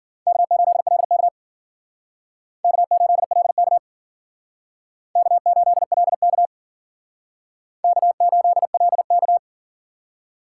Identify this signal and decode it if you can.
Morse code, heard as a keyed tone, about 50 wpm, 700 Hz. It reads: K8LK K8LK K8LK K8LK